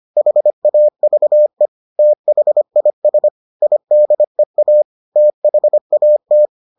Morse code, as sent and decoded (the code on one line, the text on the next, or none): .... .- ...- . / - .... .. ... / .. -.. .- / - .... .- -
HAVE THIS IDEA THAT